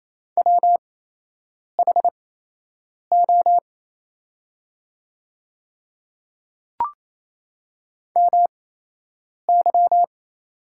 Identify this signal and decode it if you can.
Morse code, heard as a keyed tone, about 27 wpm, spaced as for 8 wpm, 700 Hz. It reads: WHO MY
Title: 8x Character Spacing - Top 100 Words - 28wpm